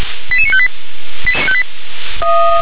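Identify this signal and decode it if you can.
Five-tone selective call (ZVEI), heard as quick runs of five tones, tones of about 70 ms. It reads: 88958
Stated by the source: Five-tone ZVEI selective call received on an SDR.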